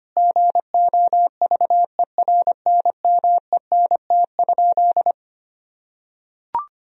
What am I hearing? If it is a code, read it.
Morse code, heard as a keyed tone, about 25 wpm, 700 Hz. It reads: GOVERNMENT?
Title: Sentences from Top 1000 Words - Review Part 3 - 25wpm